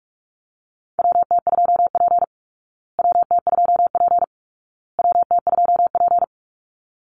Morse code, heard as a keyed tone, about 44 wpm, 700 Hz. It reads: WT2P WT2P WT2P
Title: Call Signs Repeated 2x - 45wpm